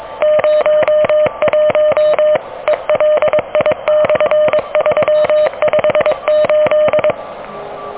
Morse code, heard as a keyed tone, about 23 wpm, 610 Hz. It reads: 01 ELI/358
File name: OK QRP 2023 / OK QRP 2023